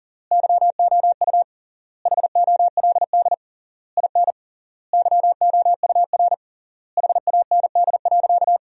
Morse code, heard as a keyed tone, about 40 wpm, 700 Hz. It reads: YOU HOLD IN YOUR HAND.
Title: 3x Word-Spacing - Sentences from Top 400 Words - 40wpm